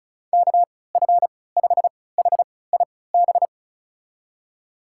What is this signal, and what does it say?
Morse code, heard as a keyed tone, about 36 wpm, spaced as for 20 wpm, 700 Hz. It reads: KF5HIB
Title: USA Call Signs - 6 Characters - 3x Character Spacing - 35wpm